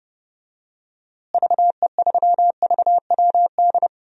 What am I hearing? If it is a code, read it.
Morse code, heard as a keyed tone, about 31 wpm, 700 Hz. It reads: VE3VWD
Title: DXCC-100 Call Signs - 6 Characters - Repeated 3x - 30wpm